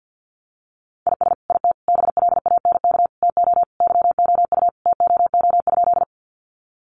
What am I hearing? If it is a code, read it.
Morse code, heard as a keyed tone, about 50 wpm, 700 Hz. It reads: IS IT BLANK TO YOU TOO?
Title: Sentences from Top 1500 Words 50wpm